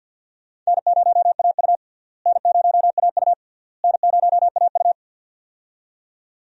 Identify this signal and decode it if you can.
Morse code, heard as a keyed tone, about 51 wpm, 700 Hz. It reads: N0AU N0AU N0AU
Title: USA Call Signs - 4 Characters - Repeated 3x - 50wpm